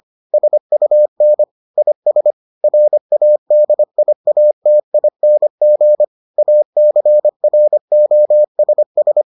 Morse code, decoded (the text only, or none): SUN IS RADIATING ACROSS